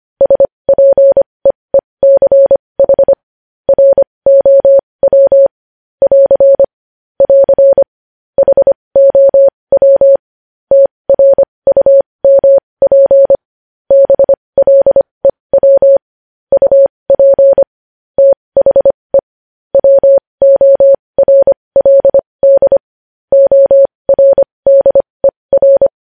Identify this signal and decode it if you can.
Morse code, heard as a keyed tone, about 25 wpm, spaced as for 20 wpm, 550 Hz. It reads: SPEECH ROW <AR> <AR> HOW TRUMP BLEW UP THE WORLD ORDER
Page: news-headlines-in-morse-code-at-20-wpm